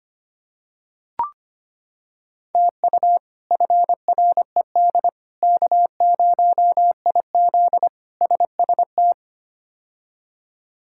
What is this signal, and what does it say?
Morse code, heard as a keyed tone, about 25 wpm, 700 Hz. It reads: TU FRED K0IZ SST
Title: K1USN SST QSO Practice - 25wpm